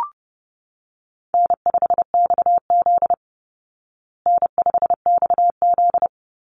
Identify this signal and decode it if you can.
Morse code, heard as a keyed tone, about 30 wpm, 700 Hz. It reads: N5XZ N5XZ
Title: Call Signs Repeated 3x - 30wpm